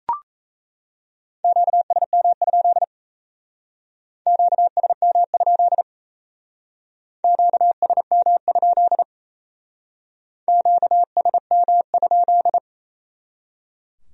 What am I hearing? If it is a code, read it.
Morse code, heard as a keyed tone, about 35 wpm, 700 Hz. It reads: QSM? QSM? QSM? QSM?
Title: Speed-Racing Q-Codes - 28wpm